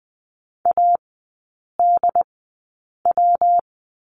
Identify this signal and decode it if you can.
Morse code, heard as a keyed tone, about 20 wpm, spaced as for 8 wpm, 700 Hz. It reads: ADW
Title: 114 Lesson 19e - Letter w - Three Characters 20wpm